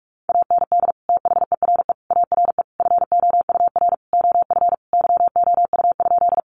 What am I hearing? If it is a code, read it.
Morse code, heard as a keyed tone, about 44 wpm, 700 Hz. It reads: AND THERE ARE FOUR OF YOU?
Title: Sentences from Top 300 Words 45wpm